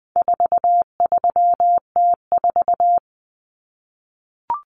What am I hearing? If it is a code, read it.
Morse code, heard as a keyed tone, about 20 wpm, 700 Hz. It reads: 43T4
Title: Common Cut Numbers - 4-Digit 20wpm